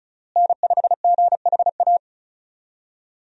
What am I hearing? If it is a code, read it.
Morse code, heard as a keyed tone, about 34 wpm, 700 Hz. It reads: N5GHA
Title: USA Call Signs - 5 Characters - Repeated 2x - 35wpm